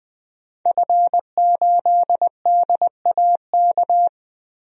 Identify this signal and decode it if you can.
Morse code, heard as a keyed tone, about 20 wpm, 700 Hz. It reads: F8DAK
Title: DXCC-100 Call Signs - 5 Characters - Repeated 3x - 20wpm